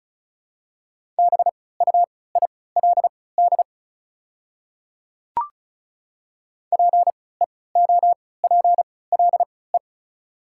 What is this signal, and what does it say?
Morse code, heard as a keyed tone, about 36 wpm, spaced as for 20 wpm, 690 Hz. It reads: BUILD PEOPLE
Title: 3x Character Spacing - Top 300 Words - 35wpm